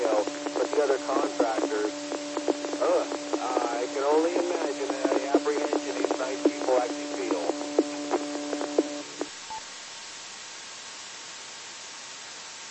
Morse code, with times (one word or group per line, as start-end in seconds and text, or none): none